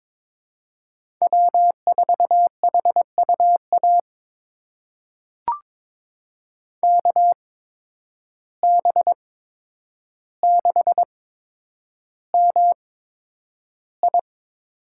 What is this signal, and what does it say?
Morse code, heard as a keyed tone, about 22 wpm, spaced as for 9 wpm, 700 Hz. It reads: W4HUA KB6MI
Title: USA Call Signs - 8x Character Spacing - 22wpm